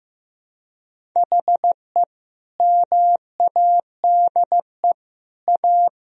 Morse code, heard as a keyed tone, about 15 wpm, 700 Hz. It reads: HE MADE A
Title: Sentences from Top 1000 Words - Review Part 1 - 15wpm